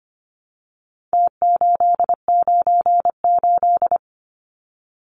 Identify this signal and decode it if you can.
Morse code, heard as a keyed tone, about 25 wpm, 700 Hz. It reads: T898
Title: Numbers - Contest Style - 4-Digit 25wpm